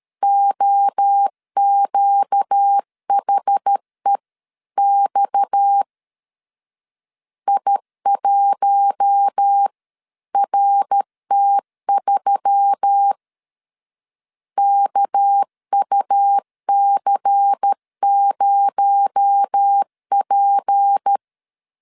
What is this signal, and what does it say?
Morse code, heard as a keyed tone, about 13 wpm, 800 Hz. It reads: OQHEX I1RT3 KUC0P
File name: CW / DET / 1 6 1500C S A 12wpm 3440